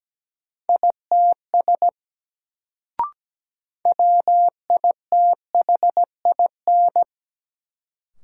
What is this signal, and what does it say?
Morse code, heard as a keyed tone, about 17 wpm, 700 Hz. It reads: ITS WITHIN
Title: Sets of 2 Words - New Word Drill - 17wpm